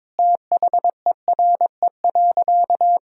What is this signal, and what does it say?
Morse code, heard as a keyed tone, about 22 wpm, 700 Hz. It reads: THERE.
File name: practice / 22wpm / Sentences with 3-Word Phrases - 22wpm